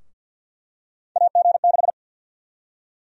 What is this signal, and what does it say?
Morse code, heard as a keyed tone, about 52 wpm, 700 Hz. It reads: AG6
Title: USA Call Sign Prefixes - Repeated 2x - 50wpm